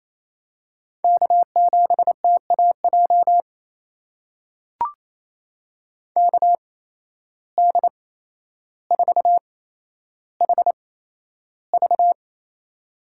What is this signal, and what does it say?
Morse code, heard as a keyed tone, about 28 wpm, spaced as for 12 wpm, 700 Hz. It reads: K7TAJ KD4HV
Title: USA Call Signs - 5 Characters - 8x Character Spacing - 28wpm